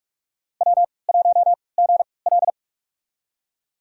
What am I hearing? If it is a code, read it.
Morse code, heard as a keyed tone, about 46 wpm, spaced as for 26 wpm, 690 Hz. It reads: W1GL
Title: USA Call Signs - 4 Characters - 3x Character Spacing - 45wpm